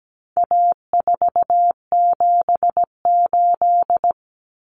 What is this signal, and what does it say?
Morse code, heard as a keyed tone, about 17 wpm, 700 Hz. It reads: A478
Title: Common Cut Numbers - 4-Digit 17wpm